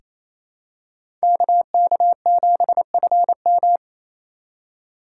Morse code, heard as a keyed tone, about 28 wpm, 700 Hz. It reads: KK7FM
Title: USA Call Signs - 4x Character Spacing - 28wpm